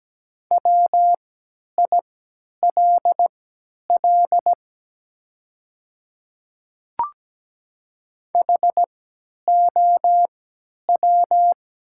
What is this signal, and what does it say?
Morse code, heard as a keyed tone, about 17 wpm, spaced as for 10 wpm, 700 Hz. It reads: WILL HOW
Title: 3x Character Spacing - Top 100 Words - 17wpm